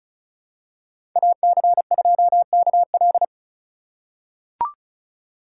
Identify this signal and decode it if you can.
Morse code, heard as a keyed tone, about 36 wpm, 700 Hz. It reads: AC2KL
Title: USA Call Signs - 5 Characters - 2x Character Spacing - 35wpm